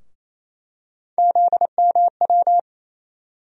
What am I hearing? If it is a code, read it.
Morse code, heard as a keyed tone, about 28 wpm, 700 Hz. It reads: ZMW
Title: USA Call Sign Postfixes - 3 Letters 28wpm